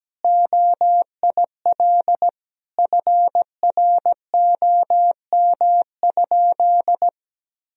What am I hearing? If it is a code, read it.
Morse code, heard as a keyed tone, about 17 wpm, 700 Hz. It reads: OIL FROM?
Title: Sentences from Top 1000 Words - Review Part 4 - 17wpm